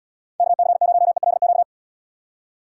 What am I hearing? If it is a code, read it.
Morse code, heard as a keyed tone, about 76 wpm, 700 Hz. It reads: KC0ELQ